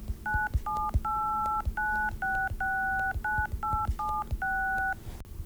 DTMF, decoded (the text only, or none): 9789669876